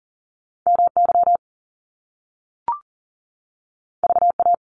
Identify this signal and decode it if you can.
Morse code, heard as a keyed tone, about 40 wpm, 700 Hz. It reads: MY VA
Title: Top Two Letter Combinations in Words 40wpm